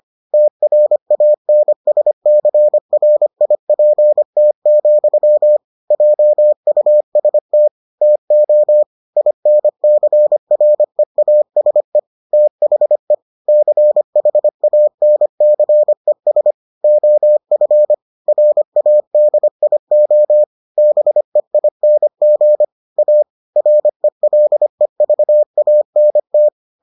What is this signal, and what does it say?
Morse code, heard as a keyed tone, about 25 wpm, 600 Hz. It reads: TRANSCRIPT, JUST TO INCREASE THE CHANCES OF RADIO BEING A RELEVANT